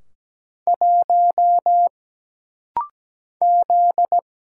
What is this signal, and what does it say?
Morse code, heard as a keyed tone, about 17 wpm, 700 Hz. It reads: 1 Z
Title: Single Letter-Number Alt - Rapid-Fire 17wpm